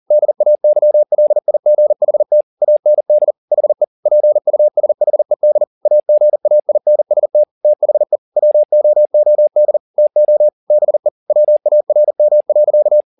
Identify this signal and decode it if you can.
Morse code, heard as a keyed tone, about 40 wpm, 600 Hz. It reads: DAYLIGHT AND HE PUSHED AGAINST THE WOOD TO BE WARM.